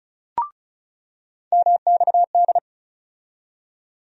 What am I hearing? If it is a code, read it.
Morse code, heard as a keyed tone, about 36 wpm, 700 Hz. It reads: MXD
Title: USA Call Sign Postfixes - 3 Letters 35wpm